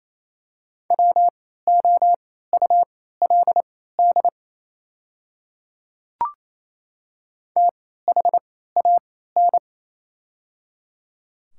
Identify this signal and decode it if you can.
Morse code, heard as a keyed tone, about 29 wpm, spaced as for 16 wpm, 690 Hz. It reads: WOULD THAN
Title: 3x Character Spacing - Top 100 Words - 28wpm